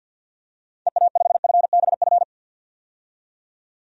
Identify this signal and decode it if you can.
Morse code, heard as a keyed tone, about 53 wpm, 700 Hz. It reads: EA5FBF
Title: DXCC-100 Call Signs - 6 Characters - 50wpm